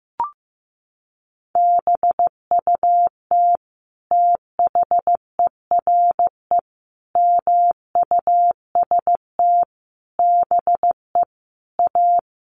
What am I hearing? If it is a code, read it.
Morse code, heard as a keyed tone, about 15 wpm, 700 Hz. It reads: BUT THERE MUST BE A
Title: Sentences from Top 1500 Words 15wpm